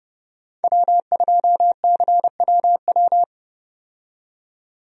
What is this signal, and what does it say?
Morse code, heard as a keyed tone, about 30 wpm, 700 Hz. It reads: W2CWW
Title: USA Call Signs - 5 Characters - 30wpm